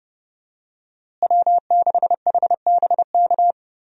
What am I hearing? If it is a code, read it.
Morse code, heard as a keyed tone, about 31 wpm, 700 Hz. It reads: W6HBK
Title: USA Call Signs - 5 Characters - Repeated 3x - 30wpm